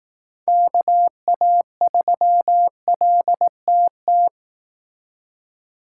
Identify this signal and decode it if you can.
Morse code, heard as a keyed tone, about 18 wpm, 700 Hz. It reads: KA3LTT
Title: Speed Racing - USA 2x3 Call Signs 15wpm